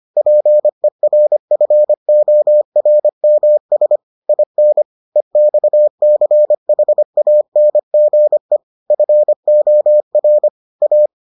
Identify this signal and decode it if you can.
Morse code, heard as a keyed tone, about 25 wpm, 600 Hz. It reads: PERFORMS IN EXCHANGE FOR A